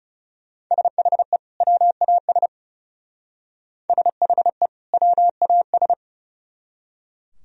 Morse code, heard as a keyed tone, about 34 wpm, 700 Hz. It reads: SHE WAS SHE WAS
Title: Speed-Racing - Sets of 2 Words 30wpm